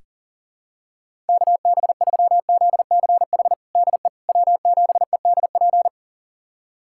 Morse code, heard as a keyed tone, about 37 wpm, 700 Hz. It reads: KB3ZCH DE W7EDP
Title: USA Callsign DE Callsign 40wpm